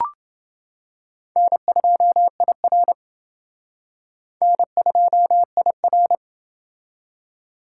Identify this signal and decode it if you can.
Morse code, heard as a keyed tone, about 28 wpm, 700 Hz. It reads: N2IR N2IR